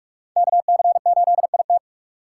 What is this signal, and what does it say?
Morse code, heard as a keyed tone, about 46 wpm, 700 Hz. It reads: KC8IT
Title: USA Call Signs - 5 Characters - 3x Character Spacing - 45wpm